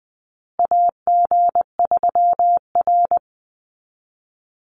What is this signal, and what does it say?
Morse code, heard as a keyed tone, about 20 wpm, 700 Hz. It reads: AG3R
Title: Call Signs Repeated 3x - 20wpm